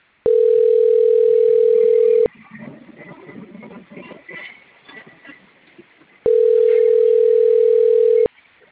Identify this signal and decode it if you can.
Ringback tone (North America), 440 + 480 Hz.